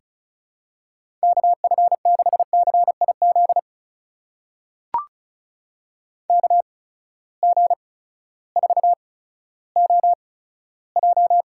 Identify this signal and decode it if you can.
Morse code, heard as a keyed tone, about 36 wpm, spaced as for 16 wpm, 700 Hz. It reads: KF6CIZ KG4OJ
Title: USA Call Signs - 6 Characters - 8x Character Spacing - 35wpm